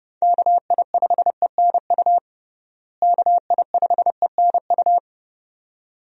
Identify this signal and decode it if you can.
Morse code, heard as a keyed tone, about 30 wpm, 700 Hz. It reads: KI5ENU KI5ENU
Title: USA Call Signs - 6 Characters - Repeated 2x - 30wpm